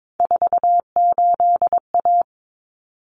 Morse code, heard as a keyed tone, about 22 wpm, 700 Hz. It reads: 48A
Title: Common Cut Numbers - 3-Digit 22wpm